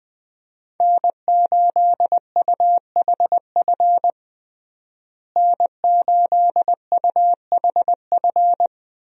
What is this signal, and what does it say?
Morse code, heard as a keyed tone, about 20 wpm, 700 Hz. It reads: N8UHF N8UHF
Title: USA Call Signs - 5 Characters - Repeated 2x - 20wpm